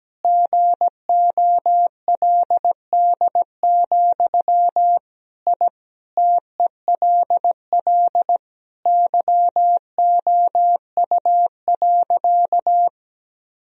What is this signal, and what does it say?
Morse code, heard as a keyed tone, about 17 wpm, 700 Hz. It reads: GOLD, I TELL YOU.
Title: Sentences from Top 1000 Words - Review Part 3 - 17wpm